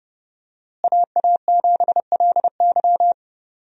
Morse code, heard as a keyed tone, about 32 wpm, 700 Hz. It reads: AA7LY
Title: USA Call Signs - 5 Characters - Repeated 3x - 30wpm